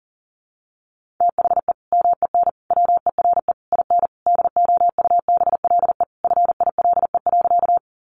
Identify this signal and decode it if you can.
Morse code, heard as a keyed tone, about 40 wpm, 700 Hz. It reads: THE MEN WERE IN DOUBLE FILE.
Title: Sentences from Top 1400 Words 40wpm